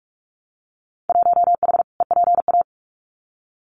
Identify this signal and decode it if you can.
Morse code, heard as a keyed tone, about 46 wpm, 700 Hz. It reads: 1H EPA
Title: Winter Field Day Exchanges 45wpm